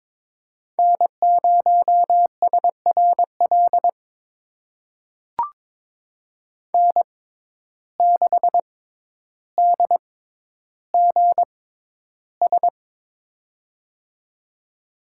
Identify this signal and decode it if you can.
Morse code, heard as a keyed tone, about 23 wpm, spaced as for 11 wpm, 700 Hz. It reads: N0SRL N6DGS
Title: USA Call Signs - 5 Characters - 6x Character Spacing - 22wpm